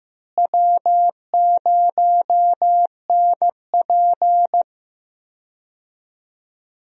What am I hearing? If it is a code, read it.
Morse code, heard as a keyed tone, about 15 wpm, 700 Hz. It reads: W0NP
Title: USA Call Signs - 4 Characters - Repeated 2x - 15wpm